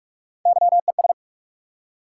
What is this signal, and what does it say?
Morse code, heard as a keyed tone, about 44 wpm, 700 Hz. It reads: YES
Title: Three Letter Words - 45wpm